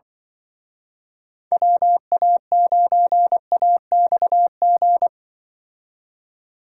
Morse code, heard as a keyed tone, about 24 wpm, 700 Hz. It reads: WA9AXG
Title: Speed Racing - USA Call Signs 20wpm